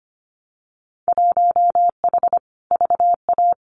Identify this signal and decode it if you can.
Morse code, heard as a keyed tone, about 25 wpm, 700 Hz. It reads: 1H VA
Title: Winter Field Day Exchanges 25wpm